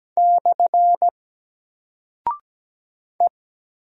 Morse code, heard as a keyed tone, about 17 wpm, 700 Hz. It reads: / E
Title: Single Character (Letter:Number:Punctuation:Prosign) Alt 17wpm